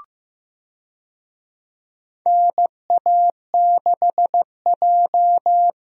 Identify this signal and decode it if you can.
Morse code, heard as a keyed tone, about 15 wpm, 700 Hz. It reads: NA6J
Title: USA Call Signs - 4 Characters - Repeated 3x - 15wpm